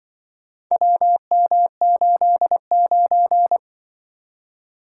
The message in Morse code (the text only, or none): WM89